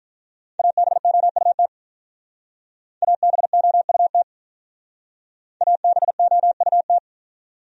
Morse code, heard as a keyed tone, about 44 wpm, 700 Hz. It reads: ABOUT ABOUT ABOUT
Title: Speed-Racing Top 100 Words 35wpm